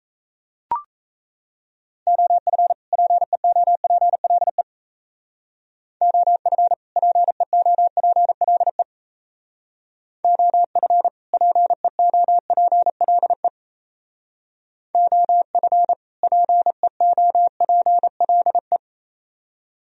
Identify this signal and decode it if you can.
Morse code, heard as a keyed tone, about 33 wpm, 700 Hz. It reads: OF PEOPLE OF PEOPLE OF PEOPLE OF PEOPLE
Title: Speed-Racing - Sets of 2 Words 28wpm